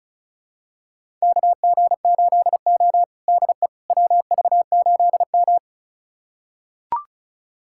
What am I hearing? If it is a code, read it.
Morse code, heard as a keyed tone, about 36 wpm, 700 Hz. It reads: KG8O DE WV8M
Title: USA Callsign DE Callsign - 4 Characters - 35wpm